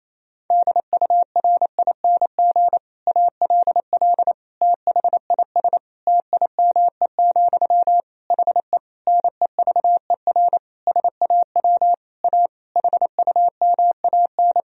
Morse code, heard as a keyed tone, about 28 wpm, 700 Hz. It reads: DURING ALL THIS TIME, HE NEVER SAW A HUMAN
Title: Sentences from Top 1000 Words - Review Part 3 - 28wpm